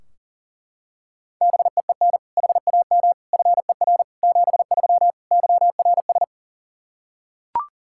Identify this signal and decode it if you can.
Morse code, heard as a keyed tone, about 40 wpm, 700 Hz. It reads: BEEN HAM FER 73 YRS
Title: Practice QSO - Exchange 3 - More Info - 40wpm